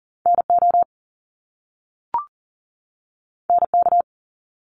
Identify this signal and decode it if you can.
Morse code, heard as a keyed tone, about 42 wpm, 700 Hz. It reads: NO NK
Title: Top Two Letter Combinations in Words 40wpm